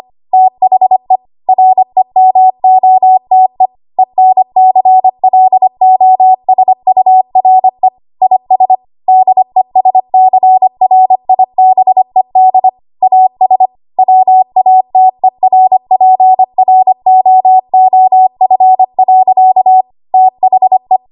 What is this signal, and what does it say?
Morse code, heard as a keyed tone, about 25 wpm, 750 Hz. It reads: THE REMOTE ENCLOSURE IS DESCRIBED AS WATERPROOF. THE
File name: Archive / 25 WPM / 25WPM